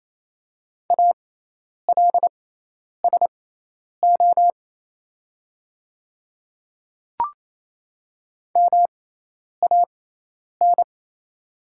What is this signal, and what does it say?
Morse code, heard as a keyed tone, about 28 wpm, spaced as for 10 wpm, 700 Hz. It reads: ALSO MAN
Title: 6x Character Spacing - Top 200 Words - 28wpm